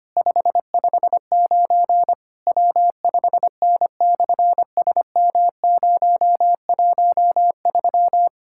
Morse code, heard as a keyed tone, about 25 wpm, 700 Hz. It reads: 559 W5N/SM013
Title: SOTA QSO Practice 25wpm